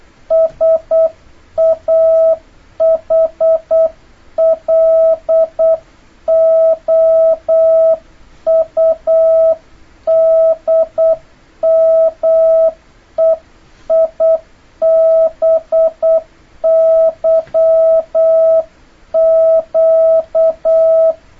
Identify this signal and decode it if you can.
Morse code, heard as a keyed tone, about 8 wpm, 650 Hz. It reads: SAHLOUDMEIBYQ
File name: jh8cbh / 2026cw